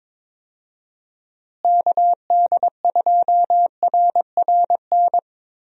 Morse code, heard as a keyed tone, about 22 wpm, 700 Hz. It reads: KD2RRN
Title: USA Call Signs - 6 Characters - Repeated 3x - 22wpm